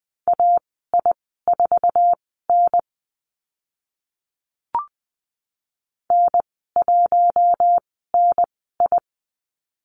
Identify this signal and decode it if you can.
Morse code, heard as a keyed tone, about 20 wpm, spaced as for 14 wpm, 700 Hz. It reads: AI4N N1NI
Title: 076 Lesson 13g - Letter L - Call Signs 20wpm